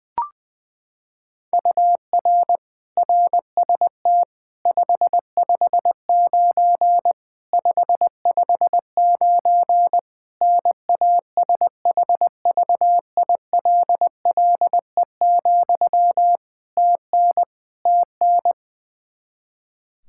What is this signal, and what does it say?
Morse code, heard as a keyed tone, about 20 wpm, 700 Hz. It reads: UR RST 559 559 NASHVILLE, TN TN
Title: Practice QSO - Exchange 1 - Beginning - 20wpm